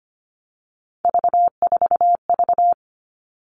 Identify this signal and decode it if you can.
Morse code, heard as a keyed tone, about 25 wpm, 700 Hz. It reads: V4V
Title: All Cut Numbers - 3-Digit 25wpm